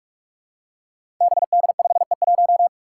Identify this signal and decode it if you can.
Morse code, heard as a keyed tone, about 44 wpm, 700 Hz. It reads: BD5E1
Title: Five Challenging Letter-Number Combinations 45wpm